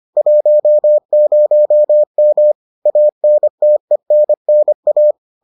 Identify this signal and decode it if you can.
Morse code, heard as a keyed tone, about 25 wpm, 600 Hz. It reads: 10M ANTENNA